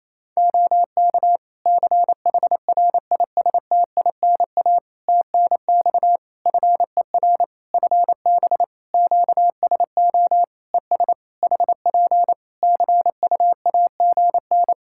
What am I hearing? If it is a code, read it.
Morse code, heard as a keyed tone, about 28 wpm, 700 Hz. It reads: OK CHRISTINA TNX FER FB QSO ES HP CUAGN
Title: Practice QSO - Exchange 4 - Ending - 28wpm